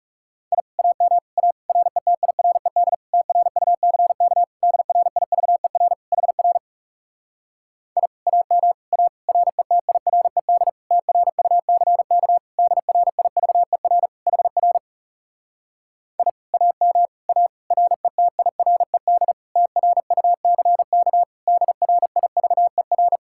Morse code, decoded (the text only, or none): I AM A RETIRED TRUCK DRIVER HR I AM A RETIRED TRUCK DRIVER HR I AM A RETIRED TRUCK DRIVER